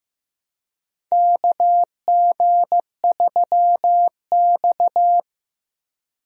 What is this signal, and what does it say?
Morse code, heard as a keyed tone, about 15 wpm, 700 Hz. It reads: KG3X
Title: USA Call Signs - 4 Characters - 15wpm